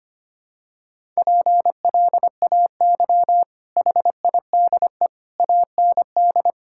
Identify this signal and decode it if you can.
Morse code, heard as a keyed tone, about 25 wpm, 690 Hz. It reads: PLAY HIDE AND
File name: Easy Sentences with Hints - 25wpm